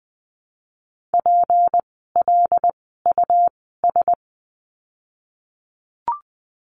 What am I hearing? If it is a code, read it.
Morse code, heard as a keyed tone, about 19 wpm, spaced as for 14 wpm, 700 Hz. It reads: PLUS
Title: 153 Lesson 25f - Letter p - Words 20wpm